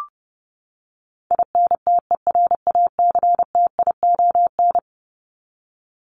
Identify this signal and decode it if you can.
Morse code, heard as a keyed tone, about 30 wpm, 700 Hz. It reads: INTERACTION
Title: Top 1251-1500 Words 30wpm